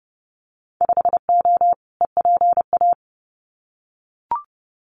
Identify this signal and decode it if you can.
Morse code, heard as a keyed tone, about 30 wpm, 700 Hz. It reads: 5O EPA
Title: Winter Field Day Exchanges Encore 30wpm